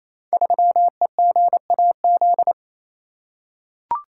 Morse code, heard as a keyed tone, about 28 wpm, 700 Hz. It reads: 3EGAZ
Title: Code Groups - 5 Letter:Numbers - 28wpm